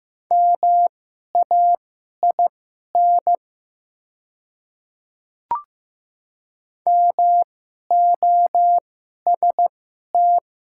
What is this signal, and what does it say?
Morse code, heard as a keyed tone, about 15 wpm, spaced as for 11 wpm, 700 Hz. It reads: MAIN MOST
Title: 2x Character Spacing - Top 300 Words - 15wpm